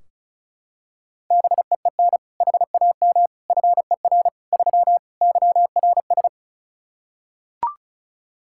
Morse code, heard as a keyed tone, about 34 wpm, 700 Hz. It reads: BEEN HAM FER 3 YRS
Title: K1USN SST QSO Practice 35wpm